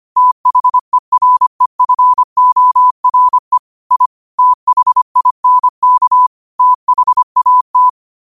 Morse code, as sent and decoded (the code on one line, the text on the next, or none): - .... .-. ..-. --- .-. / .. / - .... .. -. -.- / - .... .- -
THEREFORE I THINK THAT